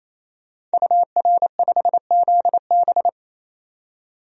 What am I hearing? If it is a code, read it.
Morse code, heard as a keyed tone, about 28 wpm, 700 Hz. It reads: UR5ZB
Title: DXCC-100 Call Signs - 5 Characters - 28wpm